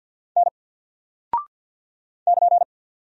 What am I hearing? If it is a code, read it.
Morse code, heard as a keyed tone, about 48 wpm, 700 Hz. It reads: N <KN>
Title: Single Character (Letter:Number:Punctuation:Prosign) 50wpm